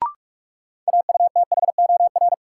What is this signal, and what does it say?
Morse code, heard as a keyed tone, about 43 wpm, 700 Hz. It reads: AUTHOR
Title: Sets of 4 Words - New Word Drill - 45wpm